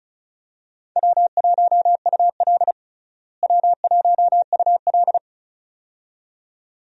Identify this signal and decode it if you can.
Morse code, heard as a keyed tone, about 36 wpm, 700 Hz. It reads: W1UL W1UL
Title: USA Call Signs - 4 Characters - Repeated 2x - 35wpm